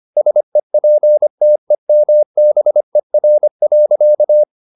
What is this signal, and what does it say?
Morse code, heard as a keyed tone, about 25 wpm, 600 Hz. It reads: SEPTEMBER.